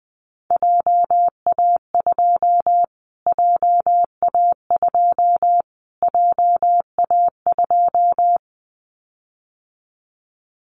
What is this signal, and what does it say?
Morse code, heard as a keyed tone, about 20 wpm, 700 Hz. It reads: JA2 JA2 JA2
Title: Callsign Prefixes for Top 25 DXCC - Repeated 3x - 20wpm